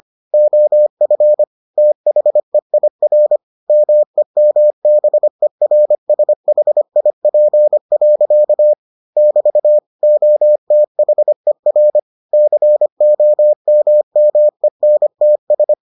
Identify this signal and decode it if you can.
Morse code, heard as a keyed tone, about 25 wpm, 600 Hz. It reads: OF THEIR MEMBERSHIP. <BT> OTHER COMMENTS